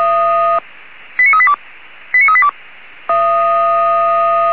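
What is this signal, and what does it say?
Five-tone selective call (ZVEI), heard as quick runs of five tones, tones of about 70 ms. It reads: 89382 89382
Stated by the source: ZVEI five-tone call received by SDR.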